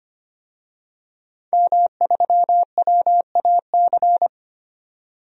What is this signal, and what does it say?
Morse code, heard as a keyed tone, about 25 wpm, 700 Hz. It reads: M3WAC
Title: DXCC-100 Call Signs - 5 Characters - 25wpm